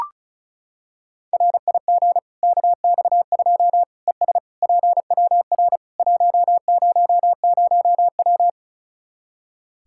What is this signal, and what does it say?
Morse code, heard as a keyed tone, about 34 wpm, 700 Hz. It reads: RIG KX2 ES PWR 100W
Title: Practice QSO - Exchange 2 - Info - 35wpm